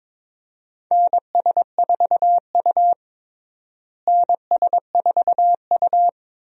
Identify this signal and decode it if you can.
Morse code, heard as a keyed tone, about 22 wpm, 700 Hz. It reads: NS4U NS4U
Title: USA Call Signs - 4 Characters - Repeated 3x - 22wpm